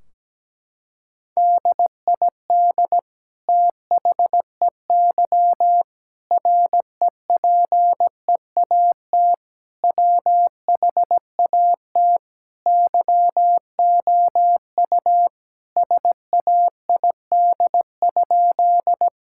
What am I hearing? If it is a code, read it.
Morse code, heard as a keyed tone, about 17 wpm, 700 Hz. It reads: DID THEY REPEAT WHAT YOU SAID?
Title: Sentences from Top 1600 Words 17wpm